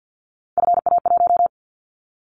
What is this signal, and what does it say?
Morse code, heard as a keyed tone, about 50 wpm, 700 Hz. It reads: FA1
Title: Callsign Prefixes for Top 50 DXCC - 50wpm